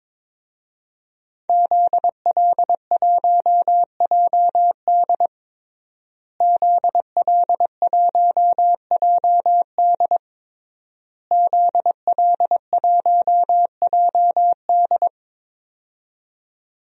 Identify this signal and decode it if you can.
Morse code, heard as a keyed tone, about 22 wpm, 700 Hz. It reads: ZL1JD ZL1JD ZL1JD
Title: DXCC-100 Call Signs - 5 Characters - Repeated 3x - 22wpm